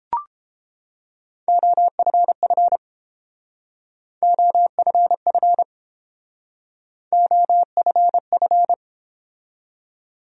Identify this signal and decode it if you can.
Morse code, heard as a keyed tone, about 31 wpm, 700 Hz. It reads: OFF OFF OFF